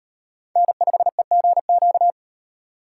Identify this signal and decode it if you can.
Morse code, heard as a keyed tone, about 35 wpm, 700 Hz. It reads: N5EGQ